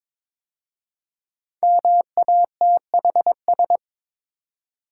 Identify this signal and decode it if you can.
Morse code, heard as a keyed tone, about 22 wpm, 700 Hz. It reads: MATHS
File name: 5 Letter Words - First 3 Letters Given - 22wpm